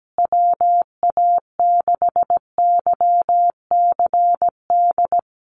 Morse code, heard as a keyed tone, about 17 wpm, 700 Hz. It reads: WA6YCD
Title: USA Call Signs Repeated 2x 17wpm